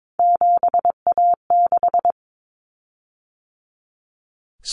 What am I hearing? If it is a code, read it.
Morse code, heard as a keyed tone, about 22 wpm, 700 Hz. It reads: 7A6
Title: Common Cut Numbers - 3-Digit 22wpm